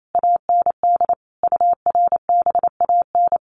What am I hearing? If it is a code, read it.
Morse code, heard as a keyed tone, about 28 wpm, 700 Hz. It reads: AND URBAN
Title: Sets of 5 Words Encore2 28wpm